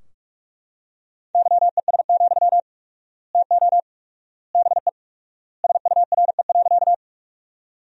Morse code, heard as a keyed tone, about 44 wpm, 700 Hz. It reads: YES, TO BE SURE.
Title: 4x Word-Spacing - Sentences from Top 500 Words - 45wpm